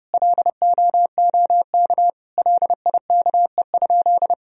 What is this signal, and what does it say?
Morse code, heard as a keyed tone, about 31 wpm, 700 Hz. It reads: LOOK LIKE?
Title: Sentences from Top 100 Words and Abbreviations 30wpm